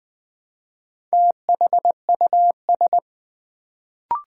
Morse code, heard as a keyed tone, about 20 wpm, 700 Hz. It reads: THUS
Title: Build A Word - 1 Letter At A Time - 4-Letter Words - 20wpm